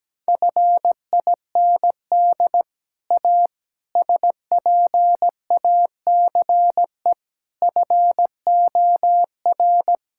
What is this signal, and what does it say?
Morse code, heard as a keyed tone, about 17 wpm, 700 Hz. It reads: FIND A SPACE FOR
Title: Sentences from Top 1000 Words - Review Part 5 - 17wpm